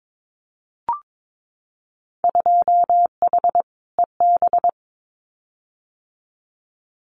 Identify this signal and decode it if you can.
Morse code, heard as a keyed tone, about 22 wpm, 700 Hz. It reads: 2H EB
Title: Winter Field Day Exchanges Encore 22wpm